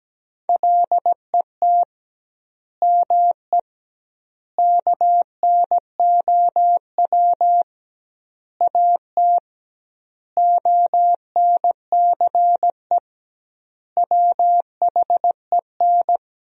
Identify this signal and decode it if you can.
Morse code, heard as a keyed tone, about 17 wpm, 700 Hz. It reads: LET ME KNOW AT ONCE WHEN
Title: 2x Word-Spacing - Sentences from Top 300 Words - 17wpm